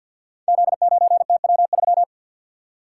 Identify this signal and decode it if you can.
Morse code, heard as a keyed tone, about 51 wpm, 700 Hz. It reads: Z9TW3
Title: Code Groups - 5 Letter:Numbers - 50wpm